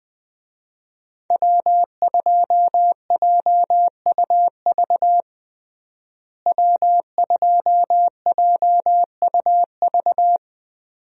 Morse code, heard as a keyed tone, about 21 wpm, 700 Hz. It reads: W2JUV W2JUV
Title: USA Call Signs - 5 Characters - Repeated 2x - 20wpm